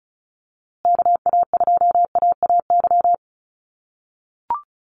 Morse code, heard as a keyed tone, about 34 wpm, 700 Hz. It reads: KA2AAY
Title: USA Call Signs Repeated 3x 35wpm